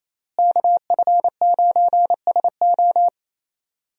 Morse code, heard as a keyed tone, about 28 wpm, 700 Hz. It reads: KF9SO
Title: USA Call Signs - 5 Characters - Repeated 3x - 28wpm